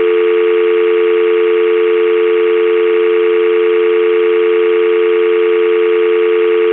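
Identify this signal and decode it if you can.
Dial tone (North America), 350 + 440 Hz, with noise.
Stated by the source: Telephone dial tone sound effect from Zvyki.com.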